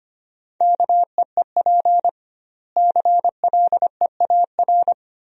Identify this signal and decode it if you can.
Morse code, heard as a keyed tone, about 25 wpm, 700 Hz. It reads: KEEP CLEAR
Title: 2x Word-Spacing - Sentences from Top 500 Words - 25wpm